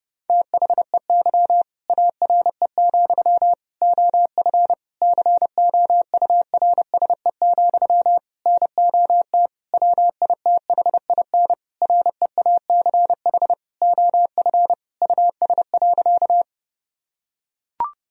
Morse code, heard as a keyed tone, about 30 wpm, 700 Hz. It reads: THEY ARE, OF COURSE, NOT WITHIN REACH OF US.
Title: Sentences from Top 1000 Words - Review Part 4 - 30wpm